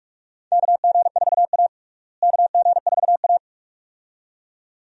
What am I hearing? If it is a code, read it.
Morse code, heard as a keyed tone, about 46 wpm, 700 Hz. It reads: KG4A KG4A
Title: USA Call Signs - 4 Characters - Repeated 2x - 45wpm